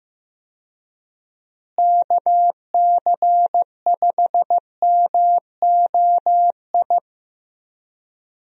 Morse code, heard as a keyed tone, about 15 wpm, 700 Hz. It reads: KC5MOI
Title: USA Call Signs - 6 Characters - Repeated 3x - 15wpm